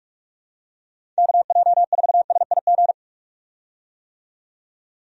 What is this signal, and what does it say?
Morse code, heard as a keyed tone, about 44 wpm, 700 Hz. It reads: KJ4SIG
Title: USA Call Signs - 6 Characters - 45wpm